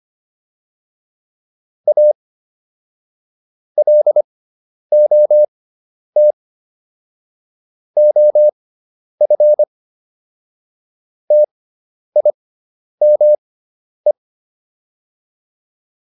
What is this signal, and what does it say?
Morse code, heard as a keyed tone, about 25 wpm, spaced as for 10 wpm, 600 Hz. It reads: A LOT OF TIME